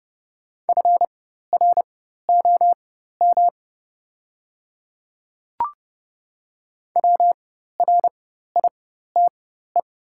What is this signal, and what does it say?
Morse code, heard as a keyed tone, about 30 wpm, spaced as for 14 wpm, 700 Hz. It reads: FROM WRITE
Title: 4x Character Spacing - Top 100 Words - 30wpm